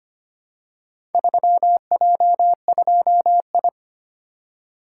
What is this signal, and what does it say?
Morse code, heard as a keyed tone, about 25 wpm, 700 Hz. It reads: 3J2I
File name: practice / 25wpm / Four Letter-Number Combinations - 25wpm